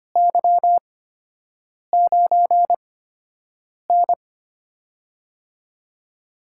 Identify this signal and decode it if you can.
Morse code, heard as a keyed tone, about 25 wpm, spaced as for 7 wpm, 700 Hz. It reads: Y9N
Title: USA Call Signs - 8x Character Spacing - 25wpm